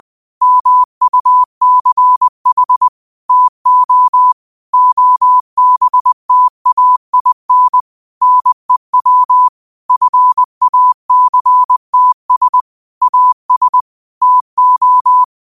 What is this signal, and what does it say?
Morse code, heard as a keyed tone, about 20 wpm, 1000 Hz. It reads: MUCH TO OBTAIN NEW FACTS AS TO
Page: 20 WPM morse code quotes for Tue, 12 Nov 2024.